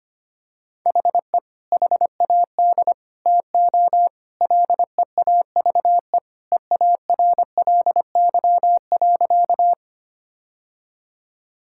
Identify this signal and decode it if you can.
Morse code, heard as a keyed tone, about 25 wpm, 700 Hz. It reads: HE HAD TO LEAVE EARLY.
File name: practice / 25wpm / Sentences with 2-Word Phrases - 25wpm